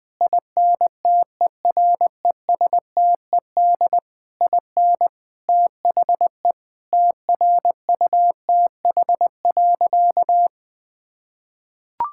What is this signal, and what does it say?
Morse code, heard as a keyed tone, about 20 wpm, 700 Hz. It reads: INTERESTED IN THE TRUTH.
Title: Sentences from Top 1000 Words - Review Part 6 - 20wpm